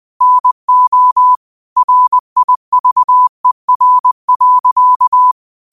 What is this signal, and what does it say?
Morse code, heard as a keyed tone, about 20 wpm, 1000 Hz. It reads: NO RIVER.